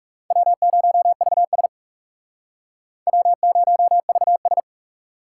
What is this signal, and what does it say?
Morse code, heard as a keyed tone, about 37 wpm, 700 Hz. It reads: W0VS W0VS
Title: Speed Racing - USA 1x2 Call Signs 30wpm